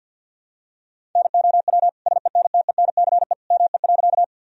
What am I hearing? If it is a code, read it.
Morse code, heard as a keyed tone, about 49 wpm, 700 Hz. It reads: NOW SENTENCE ME.
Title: Sentences from Top 1000 Words - Review Part 1 - 50wpm